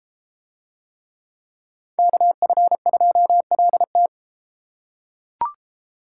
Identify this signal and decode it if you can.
Morse code, heard as a keyed tone, about 31 wpm, 700 Hz. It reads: KF2LT